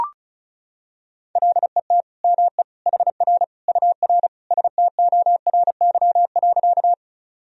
Morse code, heard as a keyed tone, about 36 wpm, 700 Hz. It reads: LET ME HR UR STORY.
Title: Sentences from Top 300 Words and Abbreviations 35wpm